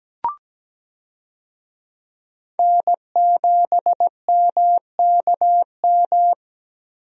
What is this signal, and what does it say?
Morse code, heard as a keyed tone, about 17 wpm, 700 Hz. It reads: N7MKM
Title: USA Call Signs - 5 Characters - Repeated 2x - 17wpm